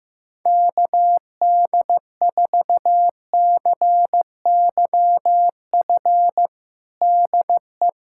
Morse code, heard as a keyed tone, about 16 wpm, 700 Hz. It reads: KD4CYF DE
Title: USA Callsign DE Callsign - 6 Characters - 15wpm